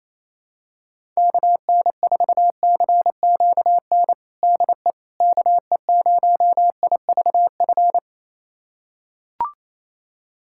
Morse code, heard as a keyed tone, about 28 wpm, 700 Hz. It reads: KN4CQN DE KE0IVF
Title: USA Callsign DE Callsign - 6 Characters - 28wpm